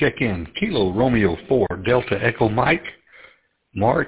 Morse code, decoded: none

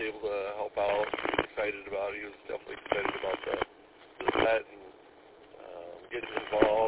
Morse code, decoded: none

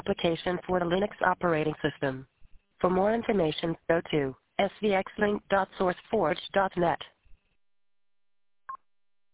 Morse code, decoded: none

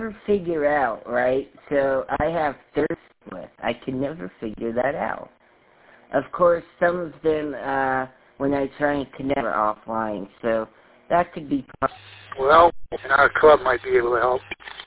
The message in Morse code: none

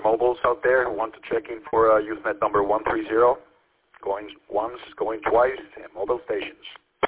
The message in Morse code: none

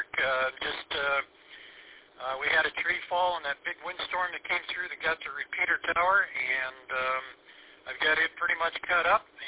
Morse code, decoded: none

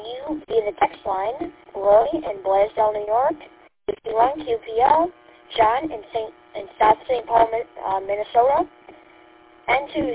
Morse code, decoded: none

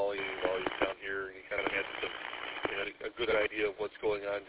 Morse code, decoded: none

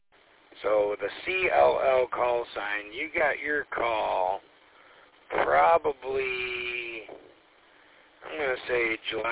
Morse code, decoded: none

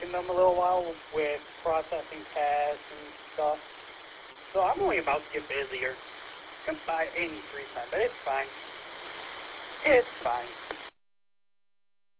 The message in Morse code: none

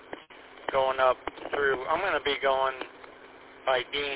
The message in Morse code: none

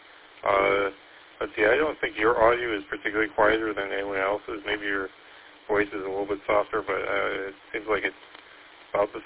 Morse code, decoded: none

